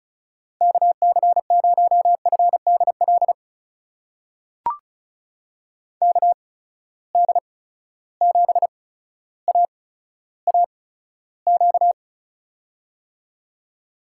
Morse code, heard as a keyed tone, about 36 wpm, spaced as for 15 wpm, 700 Hz. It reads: KC0FDL KD7AAQ
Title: USA Call Signs - 6 Characters - 8x Character Spacing - 35wpm